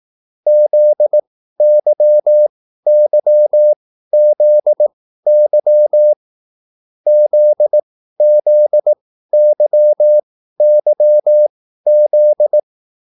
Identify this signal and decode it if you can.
Morse code, heard as a keyed tone, about 18 wpm, spaced as for 13 wpm, 600 Hz. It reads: ZYYZY ZZYYZ